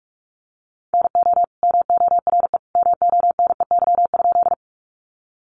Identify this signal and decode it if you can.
Morse code, heard as a keyed tone, about 44 wpm, 700 Hz. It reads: NO MORE MONEY?
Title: Sentences from Top 400 Words 45wpm